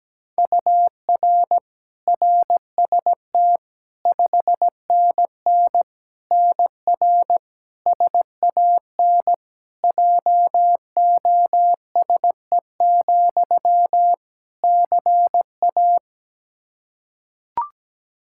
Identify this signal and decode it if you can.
Morse code, heard as a keyed tone, about 17 wpm, 700 Hz. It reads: UR RST 5NN NR SAN JOSE, CA
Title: Practice QSO - Exchange 1 - Beginning - 17wpm